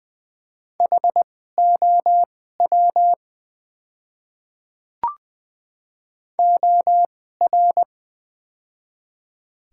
Morse code, heard as a keyed tone, about 20 wpm, spaced as for 14 wpm, 700 Hz. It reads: HOW OR
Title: 2x Character Spacing - Top 100 Words - 20wpm